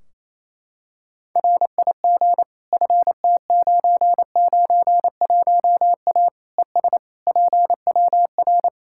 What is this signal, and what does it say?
Morse code, heard as a keyed tone, about 28 wpm, 700 Hz. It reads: RIG FT991A ES PWR
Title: K1USN SST QSO Practice 28wpm